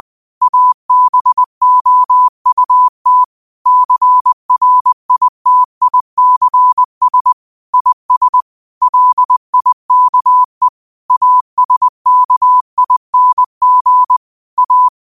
Morse code, heard as a keyed tone, about 20 wpm, 1000 Hz. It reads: ABOUT CRITICS IS LIKE ASKING A